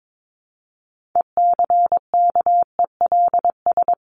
Morse code, heard as a keyed tone, about 22 wpm, 700 Hz. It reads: ECKELS
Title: A Sound Of Thunder 22wpm